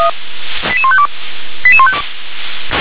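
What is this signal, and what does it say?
Five-tone selective call (ZVEI), heard as quick runs of five tones, tones of about 70 ms. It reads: 88152
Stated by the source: ZVEI five-tone selective call 88127 received with an SDR.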